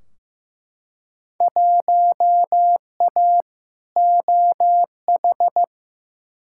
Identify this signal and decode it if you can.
Morse code, heard as a keyed tone, about 16 wpm, 700 Hz. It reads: 1A OH
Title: Field Day Exchanges - 15wpm